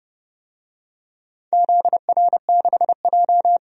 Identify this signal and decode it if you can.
Morse code, heard as a keyed tone, about 30 wpm, 700 Hz. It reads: ZR6J
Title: DXCC-100 Call Signs - 4 Characters - 30wpm